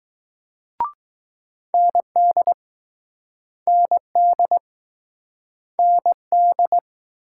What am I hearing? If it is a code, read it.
Morse code, heard as a keyed tone, about 20 wpm, 700 Hz. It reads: ND ND ND